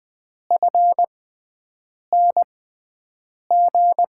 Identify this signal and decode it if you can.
Morse code, heard as a keyed tone, about 20 wpm, spaced as for 7 wpm, 700 Hz. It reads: FNG
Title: USA Call Signs - 6x Character Spacing - 20wpm